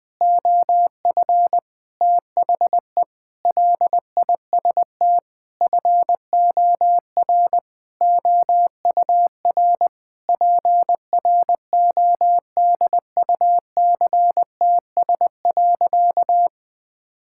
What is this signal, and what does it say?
Morse code, heard as a keyed tone, about 20 wpm, 700 Hz. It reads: OF THE LIST FOR OUR PRODUCTS.
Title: Sentences with 6-Word Phrases - Spoken Beginning - 20wpm